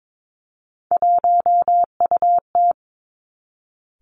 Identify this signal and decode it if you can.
Morse code, heard as a keyed tone, about 22 wpm, 700 Hz. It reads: 1UT
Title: Three Challenging Letter-Number Combinations 22wpm